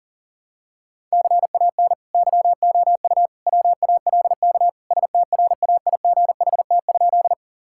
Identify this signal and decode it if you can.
Morse code, heard as a keyed tone, about 40 wpm, 700 Hz. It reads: CAN YOU WALK STRAIGHT?
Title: Sentences from Top 1000 Words - Review Part 6 - 40wpm